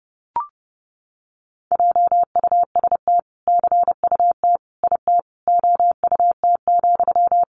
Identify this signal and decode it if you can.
Morse code, heard as a keyed tone, about 30 wpm, 700 Hz. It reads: JUST CUT IT OUT,
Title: Sentences from Top 300 Words 30wpm